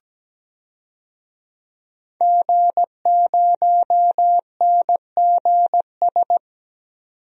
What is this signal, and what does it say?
Morse code, heard as a keyed tone, about 17 wpm, 700 Hz. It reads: G0NGS
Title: DXCC-100 Call Signs - 5 Characters - 17wpm